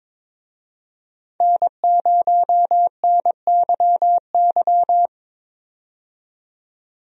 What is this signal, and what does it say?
Morse code, heard as a keyed tone, about 22 wpm, 700 Hz. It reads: N0NYY
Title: USA Call Signs - 5 Characters - 22wpm